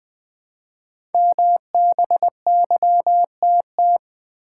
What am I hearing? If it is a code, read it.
Morse code, heard as a keyed tone, about 20 wpm, 700 Hz. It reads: MBYTT
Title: Code Groups - 5 Letter:Numbers - 20wpm